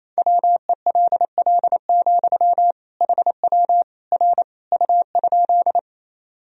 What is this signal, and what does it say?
Morse code, heard as a keyed tone, about 28 wpm, 700 Hz. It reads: WELL, HW R U?
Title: Sentences from Top 200 Words and Abbreviations 28wpm